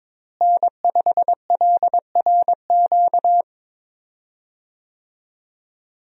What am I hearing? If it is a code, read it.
Morse code, heard as a keyed tone, about 23 wpm, 700 Hz. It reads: N5LRQ
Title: USA Call Signs - 5 Characters - Repeated 2x - 22wpm